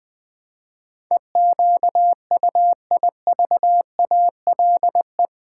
Morse code, heard as a keyed tone, about 20 wpm, 700 Hz. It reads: EQUIVALE
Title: Build A Word - 4 Letters At A Time - 10-Letter Words - 20wpm